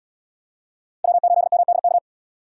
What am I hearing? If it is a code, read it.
Morse code, heard as a keyed tone, about 76 wpm, 700 Hz. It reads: W8MDK